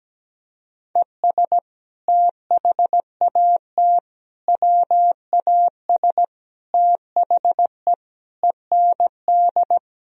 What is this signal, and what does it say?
Morse code, heard as a keyed tone, about 17 wpm, 700 Hz. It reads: ES THAT WAS THE END
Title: Sentences from Top 200 Words and Abbreviations 17wpm